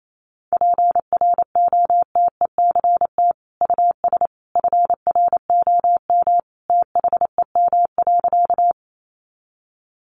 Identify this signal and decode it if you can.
Morse code, heard as a keyed tone, about 28 wpm, 700 Hz. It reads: PROTECT US FROM THEM.
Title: Sentences from Top 900 Words 28wpm